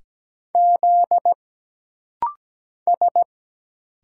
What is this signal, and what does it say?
Morse code, heard as a keyed tone, about 17 wpm, 700 Hz. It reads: Z S
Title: Single Letters - Rapid-Fire 17wpm